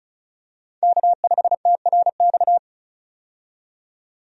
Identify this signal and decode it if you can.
Morse code, heard as a keyed tone, about 36 wpm, 700 Hz. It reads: K5TRX
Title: USA Call Signs - 5 Characters - 35wpm